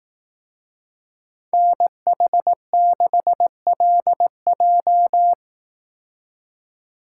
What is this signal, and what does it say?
Morse code, heard as a keyed tone, about 18 wpm, 700 Hz. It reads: NH6LJ
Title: Speed Racing - USA 2x2 Call Signs 15wpm